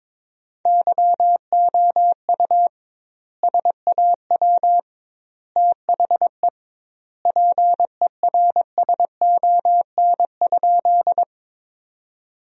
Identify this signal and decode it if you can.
Morse code, heard as a keyed tone, about 22 wpm, 700 Hz. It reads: YOU SAW THE PERSON?
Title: 2x Word-Spacing - Sentences from Top 400 Words - 22wpm